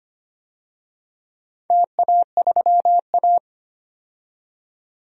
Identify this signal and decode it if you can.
Morse code, heard as a keyed tone, about 25 wpm, 700 Hz. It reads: TA3A
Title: DXCC-100 Call Signs - 4 Characters - 25wpm